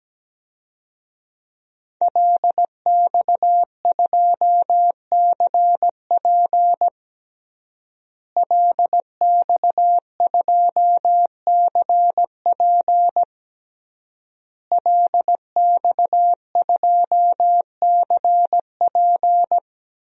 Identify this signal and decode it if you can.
Morse code, heard as a keyed tone, about 17 wpm, 700 Hz. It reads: LX2CP LX2CP LX2CP
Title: DXCC-100 Call Signs - 5 Characters - Repeated 3x - 17wpm